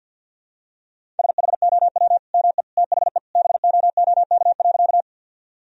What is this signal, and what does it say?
Morse code, heard as a keyed tone, about 46 wpm, 700 Hz. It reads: SHOW ME THE BOOK.
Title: Sentences from Top 1000 Words - Review Part 1 - 50wpm